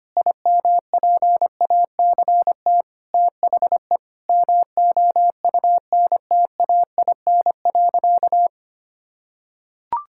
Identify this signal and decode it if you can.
Morse code, heard as a keyed tone, about 25 wpm, 700 Hz. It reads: IMPACT THE MOUNTAIN.
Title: Sentences from Top 1000 Words - Review Part 6 - 25wpm